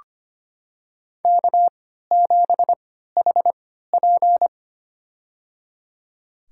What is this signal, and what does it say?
Morse code, heard as a keyed tone, about 25 wpm, spaced as for 14 wpm, 700 Hz. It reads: K7HP
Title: USA Call Signs - 4 Characters - 3x Character Spacing - 25wpm